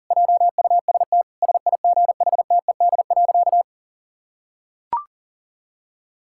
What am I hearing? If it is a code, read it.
Morse code, heard as a keyed tone, about 40 wpm, 700 Hz. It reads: JUST SIGHTED.
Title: Pangram Challenge v2 - 40wpm